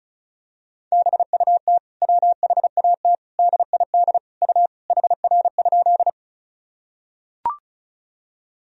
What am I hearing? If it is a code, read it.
Morse code, heard as a keyed tone, about 36 wpm, 700 Hz. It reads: BUT WHAT DID U HR?
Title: Sentences from Top 300 Words and Abbreviations 35wpm